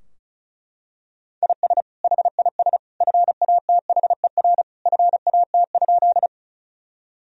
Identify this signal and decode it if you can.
Morse code, heard as a keyed tone, about 36 wpm, 700 Hz. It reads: IS HIS FATHER FAT?
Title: Sentences from Top 1000 Words - Review Part 4 - 35wpm